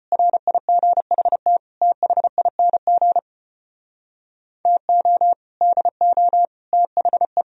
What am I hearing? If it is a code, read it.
Morse code, heard as a keyed tone, about 33 wpm, 700 Hz. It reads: RIGHT THING TO DO THE